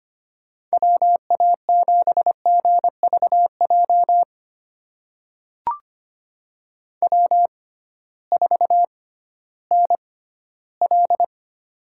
Morse code, heard as a keyed tone, about 25 wpm, spaced as for 14 wpm, 700 Hz. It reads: WA7GVJ W4NL
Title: USA Call Signs - 6x Character Spacing - 25wpm